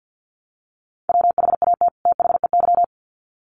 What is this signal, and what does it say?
Morse code, heard as a keyed tone, about 48 wpm, 700 Hz. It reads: WHAT THEY
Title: Sets of 2 Words 50wpm